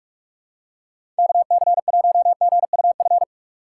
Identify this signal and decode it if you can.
Morse code, heard as a keyed tone, about 43 wpm, 700 Hz. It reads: KC1GUF
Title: USA Call Signs - 6 Characters - 2x Character Spacing - 45wpm